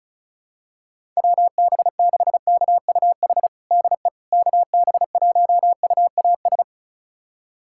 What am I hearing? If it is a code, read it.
Morse code, heard as a keyed tone, about 36 wpm, 700 Hz. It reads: WB6KUH DE KB1UAS
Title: USA Callsign DE Callsign - 6 Characters - 35wpm